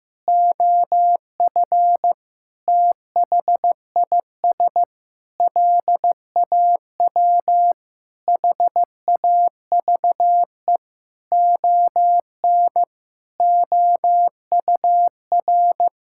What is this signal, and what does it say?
Morse code, heard as a keyed tone, about 15 wpm, 700 Hz. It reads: OF THIS LAW HAVE ON OUR
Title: Sentences from Top 1000 Words - Review Part 6 - 15wpm